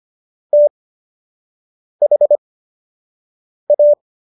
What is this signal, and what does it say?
Morse code, heard as a keyed tone, about 25 wpm, spaced as for 6 wpm, 590 Hz. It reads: THA